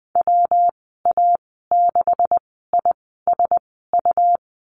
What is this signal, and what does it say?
Morse code, heard as a keyed tone, about 20 wpm, spaced as for 15 wpm, 700 Hz. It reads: WA6ISU
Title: 186 Lesson 30g - Letter b - Call Signs 20wpm